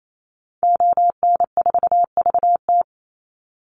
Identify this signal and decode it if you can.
Morse code, heard as a keyed tone, about 28 wpm, 700 Hz. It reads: ON4VT
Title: Call Signs - 28wpm